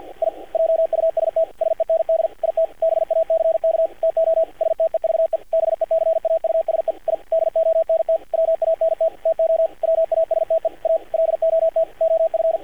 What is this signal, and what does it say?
Morse code, heard as a keyed tone, about 45 wpm, 590 Hz, with radio noise. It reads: I JUST SEND IT BACK TO STEVE BECAUSE I DONT WANT TO WASTE A LOT OF
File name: archived / CRSnet